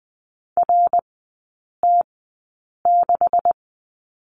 Morse code, heard as a keyed tone, about 20 wpm, spaced as for 8 wpm, 700 Hz. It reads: RT6
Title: 178 Lesson 29e - Slash - Three Characters 20wpm